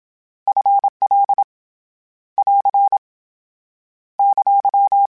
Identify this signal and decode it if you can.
Morse code, heard as a keyed tone, about 27 wpm, 800 Hz. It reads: F L <AR> !
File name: Contrasts / Contrasts-mp3s / 2LtrContrasts @25